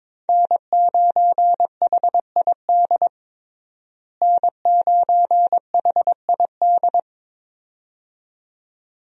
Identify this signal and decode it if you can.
Morse code, heard as a keyed tone, about 22 wpm, 700 Hz. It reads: N9HID N9HID
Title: USA Call Signs - 5 Characters - Repeated 2x - 22wpm